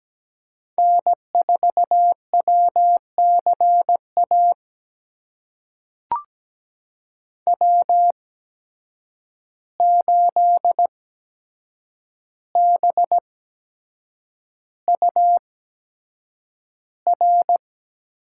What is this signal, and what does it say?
Morse code, heard as a keyed tone, about 17 wpm, spaced as for 7 wpm, 700 Hz. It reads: N4WCA W8BUR
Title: USA Call Signs - 5 Characters - 8x Character Spacing - 17wpm